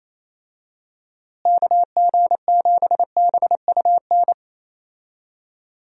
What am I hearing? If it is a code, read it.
Morse code, heard as a keyed tone, about 28 wpm, 700 Hz. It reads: KG7BUN
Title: USA Call Signs - 6 Characters - Repeated 2x - 28wpm